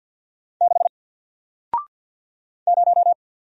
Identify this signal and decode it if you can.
Morse code, heard as a keyed tone, about 48 wpm, 700 Hz. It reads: 6 0
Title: Single Letter-Number Alt 50wpm